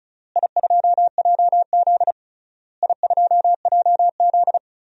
Morse code, heard as a keyed tone, about 36 wpm, 700 Hz. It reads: I2JZ I2JZ
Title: DXCC-100 Call Signs - 4 Characters - Repeated 3x - 35wpm